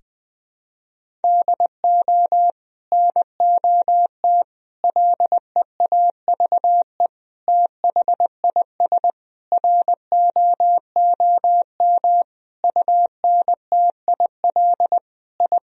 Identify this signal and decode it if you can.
Morse code, heard as a keyed tone, about 21 wpm, 700 Hz. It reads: DO NOT LEAVE THIS ROOM UNTIL I
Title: Sentences from Top 1000 Words - Review Part 1 - 20wpm